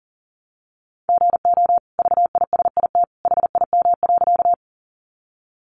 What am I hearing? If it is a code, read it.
Morse code, heard as a keyed tone, about 40 wpm, 700 Hz. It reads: GO VISIT HIM.
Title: Sentences from Top 900 Words 40wpm